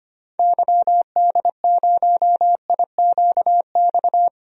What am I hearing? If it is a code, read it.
Morse code, heard as a keyed tone, about 25 wpm, 700 Hz. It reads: YD0IQX
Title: DXCC-100 Call Signs - 6 Characters - 25wpm